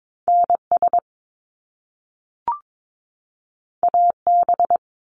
Morse code, heard as a keyed tone, about 22 wpm, 700 Hz. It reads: NS AB
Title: Canadian Provinces and Territories 22wpm